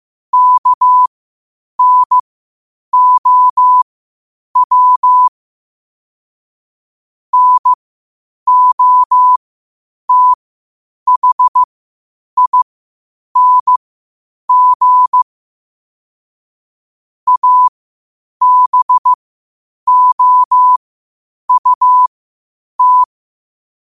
Morse code, heard as a keyed tone, about 15 wpm, spaced as for 8 wpm, 1000 Hz. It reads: KNOW NOTHING ABOUT
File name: qotdfiles / 8wpm